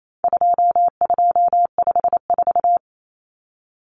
Morse code, heard as a keyed tone, about 28 wpm, 700 Hz. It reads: 2254
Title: Common Cut Numbers - 4-Digit 28wpm